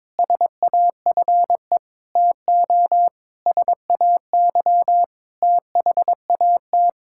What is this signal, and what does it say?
Morse code, heard as a keyed tone, about 22 wpm, 700 Hz. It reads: SAFE TO SAY THAT
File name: practice / 22wpm / Sets of 8 Words - 22wpm